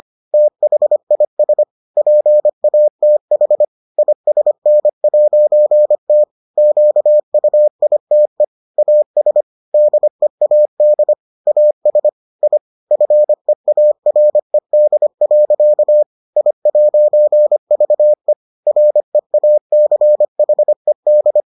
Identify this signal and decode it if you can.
Morse code, heard as a keyed tone, about 25 wpm, 600 Hz. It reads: THIS PATH ISN'T QUITE AS DEAD AS I FEARED. I'VE REACHED